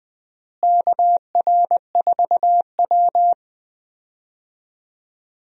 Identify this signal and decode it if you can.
Morse code, heard as a keyed tone, about 20 wpm, 700 Hz. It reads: KR4W
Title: USA Call Signs - 4 Characters - Repeated 3x - 20wpm